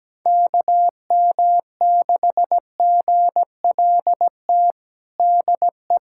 Morse code, heard as a keyed tone, about 17 wpm, 700 Hz. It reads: KM6GLT DE
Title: USA Callsign DE Callsign 17wpm